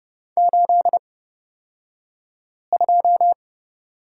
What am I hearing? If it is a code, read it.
Morse code, heard as a keyed tone, about 30 wpm, spaced as for 5 wpm, 700 Hz. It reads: 82
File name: Lesson 37d - Letter x - Two Characters 30wpm